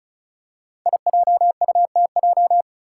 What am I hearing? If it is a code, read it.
Morse code, heard as a keyed tone, about 34 wpm, 700 Hz. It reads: IJUTJ
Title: Five Challenging Letter-Number Combinations 35wpm